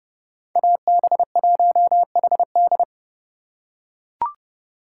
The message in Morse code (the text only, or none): AB1HD